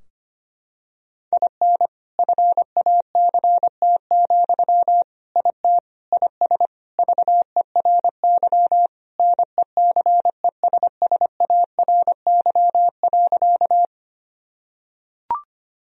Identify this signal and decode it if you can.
Morse code, heard as a keyed tone, about 25 wpm, 700 Hz. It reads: IN FACT, IT IS VERY NECESSARY.
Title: Sentences from Top 1000 Words - Review Part 3 - 25wpm